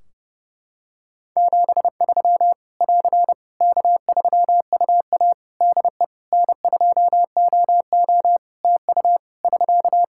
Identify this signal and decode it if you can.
Morse code, heard as a keyed tone, about 30 wpm, 700 Hz. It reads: 73 <AR> K3UA DE N2OO TU <SK>
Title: K1USN SST QSO Practice 30wpm